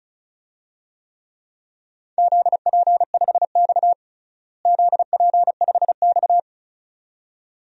Morse code, heard as a keyed tone, about 34 wpm, 700 Hz. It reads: ZP5X ZP5X
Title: DXCC-100 Call Signs - 4 Characters - Repeated 2x - 35wpm